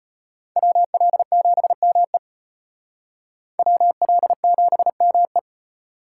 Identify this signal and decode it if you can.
Morse code, heard as a keyed tone, about 32 wpm, 700 Hz. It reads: WL7ME WL7ME